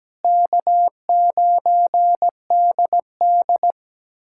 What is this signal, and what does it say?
Morse code, heard as a keyed tone, about 17 wpm, 700 Hz. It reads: K9DD
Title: USA Callsign DE Callsign - 4 Characters - 17wpm